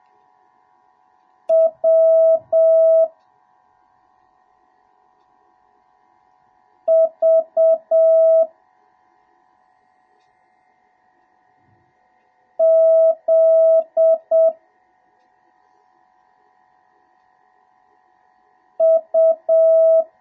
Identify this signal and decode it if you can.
Morse code, heard as a keyed tone, about 7 wpm, 650 Hz. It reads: W V Z U